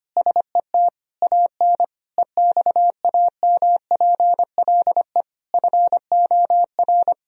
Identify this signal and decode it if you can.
Morse code, heard as a keyed tone, about 25 wpm, 700 Hz. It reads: SET AN EXAMPLE FOR
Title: Sentences from Top 1000 Words - Review Part 1 - 25wpm